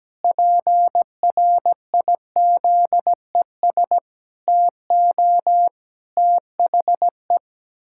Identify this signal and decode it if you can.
Morse code, heard as a keyed tone, about 17 wpm, 700 Hz. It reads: PRIZES TO THE
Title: Pangram Challenge v2 - 17wpm